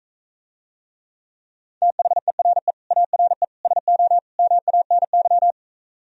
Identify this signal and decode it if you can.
Morse code, heard as a keyed tone, about 45 wpm, 700 Hz. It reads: THERE ARE SO MANY